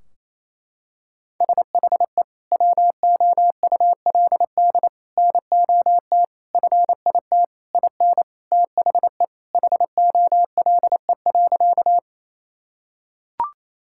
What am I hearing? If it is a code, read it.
Morse code, heard as a keyed tone, about 28 wpm, 700 Hz. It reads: SHE WOULD NOT FIT IN THE HOLE.
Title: Sentences from Top 1600 Words 28wpm